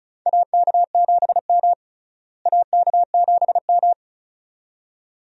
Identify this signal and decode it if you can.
Morse code, heard as a keyed tone, about 36 wpm, 700 Hz. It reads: AK7M AK7M
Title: USA Call Signs - 4 Characters - Repeated 2x - 35wpm